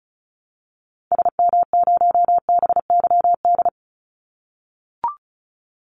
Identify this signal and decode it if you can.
Morse code, heard as a keyed tone, about 36 wpm, 700 Hz. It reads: SM0BYD
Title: Call Signs - 35wpm